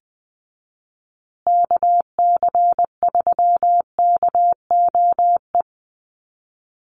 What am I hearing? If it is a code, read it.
Morse code, heard as a keyed tone, about 20 wpm, 700 Hz. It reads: KC3KOE
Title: USA Call Signs Repeated 3x 20wpm